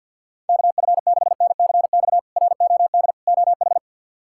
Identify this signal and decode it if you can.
Morse code, heard as a keyed tone, about 53 wpm, 700 Hz. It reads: KF6NCX ROD OH
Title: K1USN SST QSO Practice - 50wpm